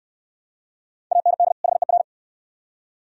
Morse code, heard as a keyed tone, about 69 wpm, 700 Hz. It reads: AND HER